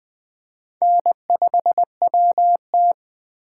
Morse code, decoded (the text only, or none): N5WT